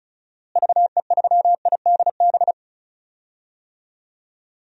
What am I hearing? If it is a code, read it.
Morse code, heard as a keyed tone, about 34 wpm, 700 Hz. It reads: VE3IDB
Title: DXCC-100 Call Signs - 6 Characters - Repeated 2x - 35wpm